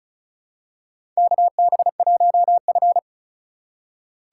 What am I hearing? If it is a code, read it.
Morse code, heard as a keyed tone, about 36 wpm, 700 Hz. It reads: KB1F
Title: USA Call Signs - 4 Characters - 4x Character Spacing - 35wpm